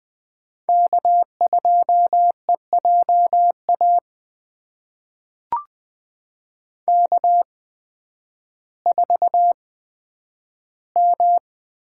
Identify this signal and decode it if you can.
Morse code, heard as a keyed tone, about 20 wpm, spaced as for 11 wpm, 700 Hz. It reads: K2EJA K4M
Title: USA Call Signs - 5 Characters - 8x Character Spacing - 20wpm